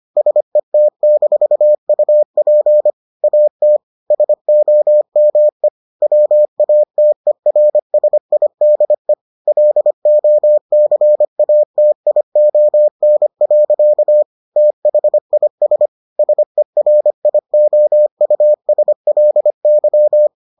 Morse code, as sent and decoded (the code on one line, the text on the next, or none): ... - -....- ..- .--. / .- - / ... --- -- . / .-- .- - . .-. ... .. -.. / .-.. --- -.-. .- - .. --- -. .-.-.- / - .... .. ... / ... .-. .. --- ..- ... .-.. -.--
SET-UP AT SOME WATERSIDE LOCATION. THIS SERIOUSLY